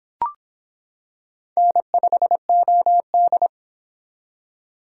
Morse code, heard as a keyed tone, about 25 wpm, 700 Hz. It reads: N5OD